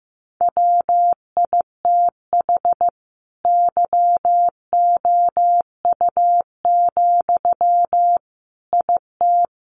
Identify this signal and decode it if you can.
Morse code, heard as a keyed tone, about 15 wpm, 700 Hz. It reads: WITH YOU, IT
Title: Sentences from Top 1400 Words 15wpm